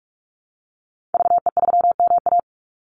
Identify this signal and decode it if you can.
Morse code, heard as a keyed tone, about 44 wpm, 700 Hz. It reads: VE3MA
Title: Call Signs - 45wpm